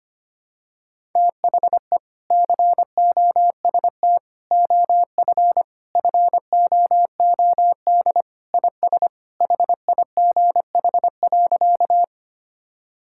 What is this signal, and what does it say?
Morse code, heard as a keyed tone, about 25 wpm, 700 Hz. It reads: THE COST OF FOOD IS HIGH.
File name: practice / 25wpm / Sentences with 3-Word Phrases - 25wpm